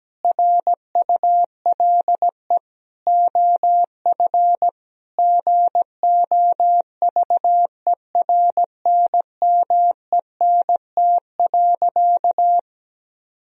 Morse code, heard as a keyed tone, about 17 wpm, 700 Hz. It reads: RULE OF GOVERNMENT.
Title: Sentences from Top 1000 Words - Review Part 3 - 17wpm